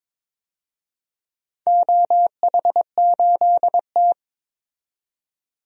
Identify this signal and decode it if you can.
Morse code, heard as a keyed tone, about 23 wpm, 700 Hz. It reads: OH8T
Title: DXCC-100 Call Signs - 4 Characters - 22wpm